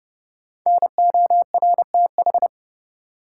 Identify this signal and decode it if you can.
Morse code, heard as a keyed tone, about 32 wpm, 700 Hz. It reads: NORTH
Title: Sentences from Top 400 Words - New Word Drill - 30wpm